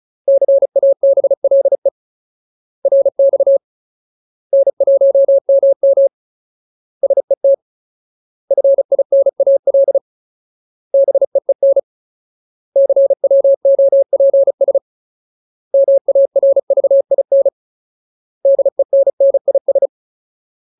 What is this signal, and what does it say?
Morse code, heard as a keyed tone, about 34 wpm, 550 Hz. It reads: CABLE RX N1MM SET FINAL BEEN CWOPS MARVIN DENNIS
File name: Ultimate Word List / 35WPM Ultimate